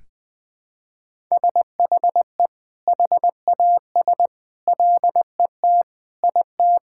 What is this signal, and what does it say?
Morse code, heard as a keyed tone, about 20 wpm, 700 Hz. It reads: SHE HAS LET IT
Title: Sentences from Top 1000 Words - Review Part 6 - 20wpm